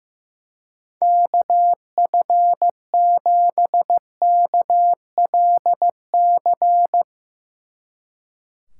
Morse code, heard as a keyed tone, about 15 wpm, 700 Hz. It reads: KF7KLC
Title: Speed Racing - USA 2x3 Call Signs 15wpm